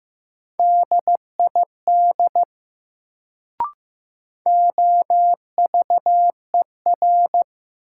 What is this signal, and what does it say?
Morse code, heard as a keyed tone, about 15 wpm, 700 Hz. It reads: DID OVER
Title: Sentences from Top 100 Words - New Word Drill - 15wpm